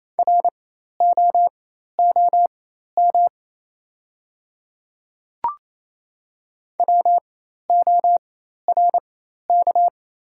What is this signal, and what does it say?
Morse code, heard as a keyed tone, about 28 wpm, spaced as for 13 wpm, 700 Hz. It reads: ROOM WORK
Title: 4x Character Spacing - Top 500 Words - 28wpm